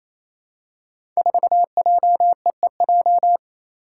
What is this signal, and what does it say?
Morse code, heard as a keyed tone, about 28 wpm, 700 Hz. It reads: 4JEEJ
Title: Five Challenging Letter-Number Combinations 28wpm